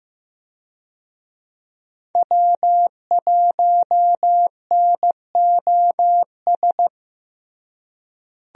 Morse code, heard as a keyed tone, about 15 wpm, 700 Hz. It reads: W1NOS
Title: USA Call Signs - 5 Characters - Repeated 2x - 15wpm